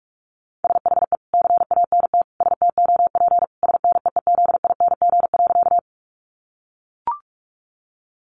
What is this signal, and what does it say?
Morse code, heard as a keyed tone, about 44 wpm, 700 Hz. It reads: SHE CANT STOP SNEEZING.
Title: Easy Phrases 45wpm